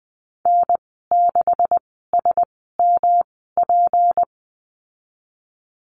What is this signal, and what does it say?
Morse code, heard as a keyed tone, about 20 wpm, spaced as for 15 wpm, 700 Hz. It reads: N6SMP
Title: 186 Lesson 30g - Letter b - Call Signs 20wpm